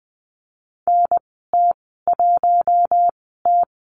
Morse code, heard as a keyed tone, about 20 wpm, spaced as for 14 wpm, 700 Hz. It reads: NT1T
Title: 042 Lesson 8g - Number 1 - Call Signs 20wpm